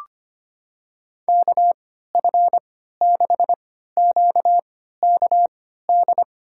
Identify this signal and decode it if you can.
Morse code, heard as a keyed tone, about 25 wpm, spaced as for 14 wpm, 690 Hz. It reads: KF6QKD
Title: USA Call Signs - 6 Characters - 3x Character Spacing - 25wpm